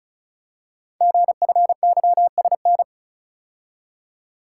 Morse code, heard as a keyed tone, about 36 wpm, 700 Hz. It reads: GFYSN
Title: Code Groups - 5 Letter:Numbers - 35wpm